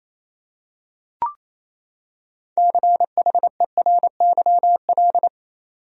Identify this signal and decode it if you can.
Morse code, heard as a keyed tone, about 28 wpm, 700 Hz. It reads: CHERYL
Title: USA Common Names - 28wpm